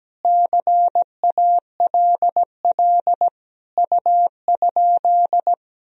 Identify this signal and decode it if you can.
Morse code, heard as a keyed tone, about 17 wpm, 700 Hz. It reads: CALL U?
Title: Sentences from Top 100 Words and Abbreviations 17wpm